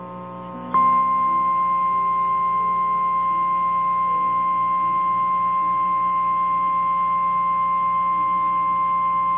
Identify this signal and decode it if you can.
Emergency Alert System (SAME) tones (the 1050 Hz attention tone). It is heard from 0.7 to 8.8 s.
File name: eas / KNCYKBIE / KNCYKBIE